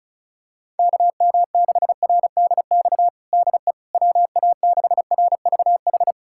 Morse code, heard as a keyed tone, about 36 wpm, 700 Hz. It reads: KM6RDX DE WA6RVH
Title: USA Callsign DE Callsign - 6 Characters - 35wpm